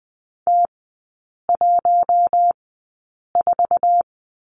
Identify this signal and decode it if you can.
Morse code, heard as a keyed tone, about 20 wpm, spaced as for 8 wpm, 700 Hz. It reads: T14
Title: 060 Lesson 11e - Letter h - Three Characters 20wpm